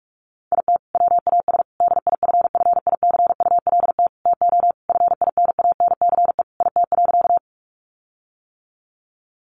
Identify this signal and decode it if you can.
Morse code, heard as a keyed tone, about 44 wpm, 700 Hz. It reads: IT WAS DIFFICULT TO FINANCE IT.
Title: Sentences from Top 1500 Words 45wpm